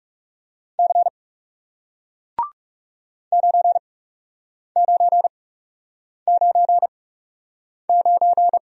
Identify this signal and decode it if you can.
Morse code, heard as a keyed tone, about 37 wpm, spaced as for 8 wpm, 700 Hz. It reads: C 9999
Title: Speed-Racing Single Letter-Number 30wpm